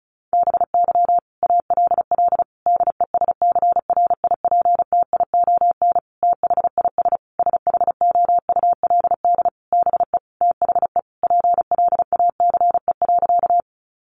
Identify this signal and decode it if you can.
Morse code, heard as a keyed tone, about 34 wpm, 700 Hz. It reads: BY ALL DESCRIPTION THIS SHOULD BE THE PLACE.
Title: Sentences from Top 1300 Words 35wpm